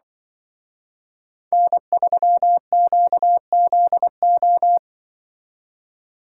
Morse code, heard as a keyed tone, about 24 wpm, 700 Hz. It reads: N3QZO
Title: Speed Racing - USA 1x3 Call Signs 20wpm